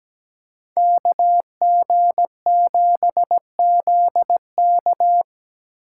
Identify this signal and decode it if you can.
Morse code, heard as a keyed tone, about 20 wpm, 700 Hz. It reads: KG7ZK
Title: Speed Racing - USA 2x2 Call Signs 17wpm